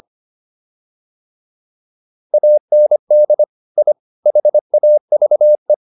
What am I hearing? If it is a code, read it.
Morse code, heard as a keyed tone, about 25 wpm, 600 Hz. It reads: AND I HAVE